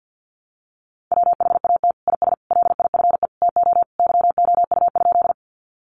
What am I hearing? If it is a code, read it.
Morse code, heard as a keyed tone, about 50 wpm, 700 Hz. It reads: WHAT IS LIFE TO YOU?
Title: Sentences from Top 300 Words 50wpm